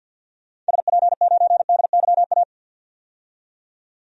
Morse code, heard as a keyed tone, about 48 wpm, 700 Hz. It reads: SP9DYA
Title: DXCC-100 Call Signs - 6 Characters - 50wpm